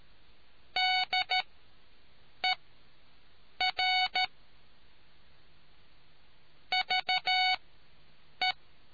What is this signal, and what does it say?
Morse code, heard as a keyed tone, about 13 wpm, spaced as for 7 wpm, 740 Hz. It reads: DER VE